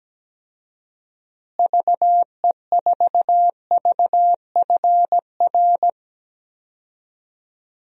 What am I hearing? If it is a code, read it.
Morse code, heard as a keyed tone, about 17 wpm, 700 Hz. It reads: VE4VFR
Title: DXCC-100 Call Signs - 6 Characters - Repeated 2x - 17wpm